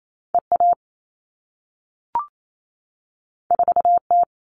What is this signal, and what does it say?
Morse code, heard as a keyed tone, about 28 wpm, 700 Hz. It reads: EA 4T
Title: All Cut Numbers - 2-Digit 28wpm